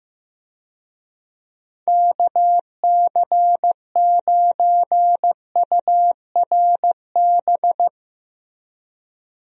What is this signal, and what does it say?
Morse code, heard as a keyed tone, about 15 wpm, 700 Hz. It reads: KC9URB
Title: USA Call Signs - 6 Characters - Repeated 2x - 15wpm